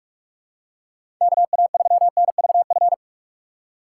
Morse code, heard as a keyed tone, about 46 wpm, 690 Hz. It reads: KA3NVF
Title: USA Call Signs - 6 Characters - 2x Character Spacing - 45wpm